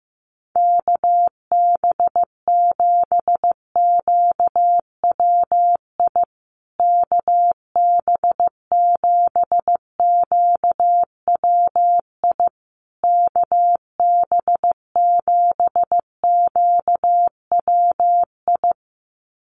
Morse code, heard as a keyed tone, about 15 wpm, 700 Hz. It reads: KB7QWI KB7QWI KB7QWI
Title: USA Call Signs Repeated 3x 15wpm